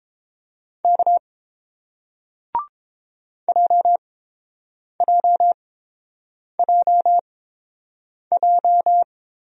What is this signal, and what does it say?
Morse code, heard as a keyed tone, about 30 wpm, spaced as for 7 wpm, 700 Hz. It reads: K JJJJ